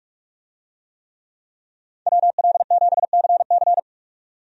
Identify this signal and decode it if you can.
Morse code, heard as a keyed tone, about 42 wpm, 700 Hz. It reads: WL7CC
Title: USA Call Signs - 5 Characters - 45wpm